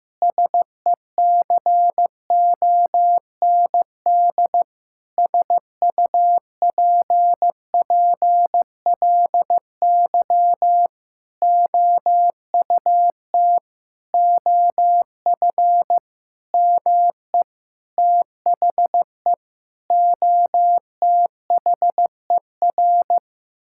Morse code, heard as a keyed tone, about 16 wpm, 700 Hz. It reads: SECOND SUPPLY OUT OF ME THE OTHER
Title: Sentences from Top 1000 Words - Review Part 5 - 15wpm